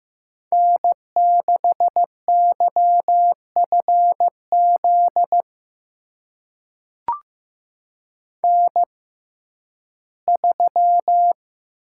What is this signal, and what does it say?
Morse code, heard as a keyed tone, about 15 wpm, spaced as for 11 wpm, 700 Hz. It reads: N6YFZ N3
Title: USA Call Signs - 5 Characters - 6x Character Spacing - 15wpm